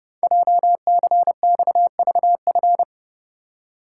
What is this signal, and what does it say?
Morse code, heard as a keyed tone, about 28 wpm, 700 Hz. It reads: JCXVF
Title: Code Groups - 5 Letters - 30wpm